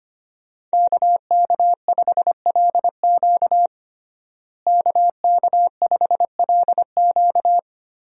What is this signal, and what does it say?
Morse code, heard as a keyed tone, about 25 wpm, 700 Hz. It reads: KK5LQ KK5LQ
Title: USA Call Signs - 5 Characters - Repeated 2x - 25wpm